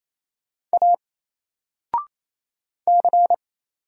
Morse code, heard as a keyed tone, about 27 wpm, 700 Hz. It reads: A C
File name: Single-Letter-Number-28wpm